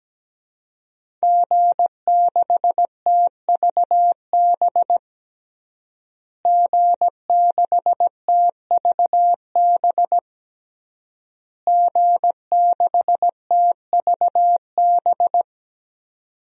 Morse code, heard as a keyed tone, about 17 wpm, 700 Hz. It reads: G6TVB G6TVB G6TVB
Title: DXCC-100 Call Signs - 5 Characters - Repeated 3x - 17wpm